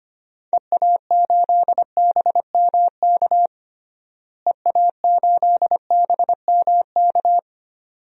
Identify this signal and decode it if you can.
Morse code, heard as a keyed tone, about 25 wpm, 700 Hz. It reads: EA8BMK EA8BMK
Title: DXCC-100 Call Signs - 6 Characters - Repeated 2x - 25wpm